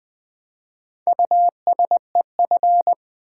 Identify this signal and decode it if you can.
Morse code, heard as a keyed tone, about 20 wpm, 700 Hz. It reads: USEF
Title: Build A Word - 2 Letters At A Time - 6-Letter Words - 20wpm